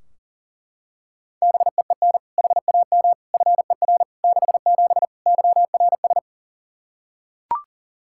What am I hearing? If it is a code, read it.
Morse code, heard as a keyed tone, about 40 wpm, 700 Hz. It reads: BEEN HAM FER 67 YRS
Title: Practice QSO - Exchange 3 - More Info - 40wpm